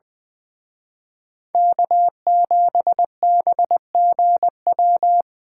Morse code, heard as a keyed tone, about 23 wpm, 700 Hz. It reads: K7BGW
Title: Speed Racing - USA 1x3 Call Signs 20wpm